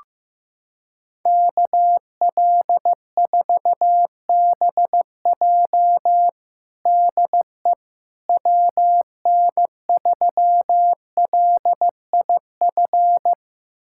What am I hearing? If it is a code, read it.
Morse code, heard as a keyed tone, about 15 wpm, 700 Hz. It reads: KL4BJ DE WN3LIF
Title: USA Callsign DE Callsign 15wpm